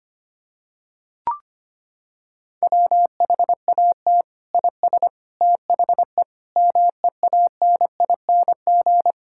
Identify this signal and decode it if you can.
Morse code, heard as a keyed tone, about 25 wpm, 700 Hz. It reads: WHAT IS THE MEANING
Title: Sentences from Top 1600 Words 25wpm